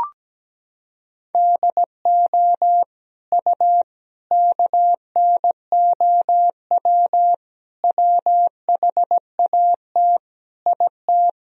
Morse code, heard as a keyed tone, about 17 wpm, 700 Hz. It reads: DO U KNOW WHAT IT
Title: Sentences from Top 100 Words and Abbreviations 17wpm